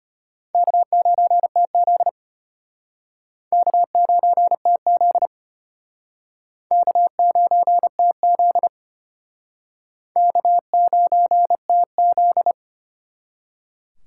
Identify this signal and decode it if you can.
Morse code, heard as a keyed tone, about 32 wpm, 700 Hz. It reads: K9TZ K9TZ K9TZ K9TZ